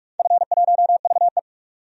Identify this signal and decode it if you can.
Morse code, heard as a keyed tone, about 45 wpm, 700 Hz. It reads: F1VE
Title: DXCC-100 Call Signs - 4 Characters - 45wpm